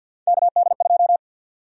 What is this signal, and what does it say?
Morse code, heard as a keyed tone, about 52 wpm, 700 Hz. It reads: KD2
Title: USA Call Sign Prefixes - 50wpm